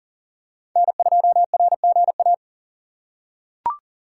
Morse code, heard as a keyed tone, about 40 wpm, 700 Hz. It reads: N2RGA
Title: USA Call Signs - 4x Character Spacing - 40wpm